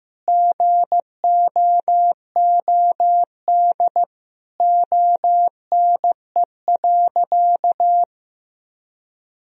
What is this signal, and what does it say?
Morse code, heard as a keyed tone, about 15 wpm, 700 Hz. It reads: GOOD ONE.
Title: Sentences from Top 1000 Words - Review Part 3 - 15wpm